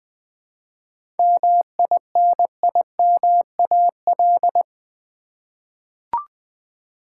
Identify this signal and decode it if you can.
Morse code, heard as a keyed tone, about 20 wpm, 700 Hz. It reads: MINIMAL
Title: 7 Letter Words - First 3 Letters Given - 20wpm